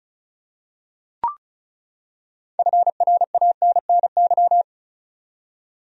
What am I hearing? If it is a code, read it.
Morse code, heard as a keyed tone, about 34 wpm, 700 Hz. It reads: FRANNY
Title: USA Common Names - 35wpm